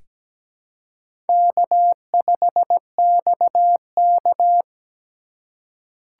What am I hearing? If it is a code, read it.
Morse code, heard as a keyed tone, about 17 wpm, 700 Hz. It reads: K5XK
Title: K1USN SST QSO Practice - 17wpm